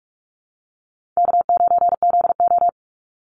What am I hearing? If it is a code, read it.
Morse code, heard as a keyed tone, about 44 wpm, 700 Hz. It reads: K9ZO
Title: Easy Call Signs Repeated 3x - 45wpm